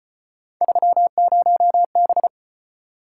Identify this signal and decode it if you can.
Morse code, heard as a keyed tone, about 34 wpm, 700 Hz. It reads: 30B